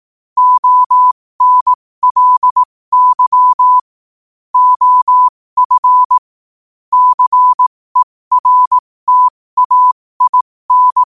Morse code, heard as a keyed tone, about 18 wpm, spaced as for 15 wpm, 1000 Hz. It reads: ONLY OF CERTAIN
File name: qotdfiles / 15wpm